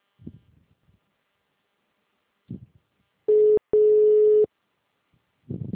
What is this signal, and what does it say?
Ringback tone (Europe, 425 Hz).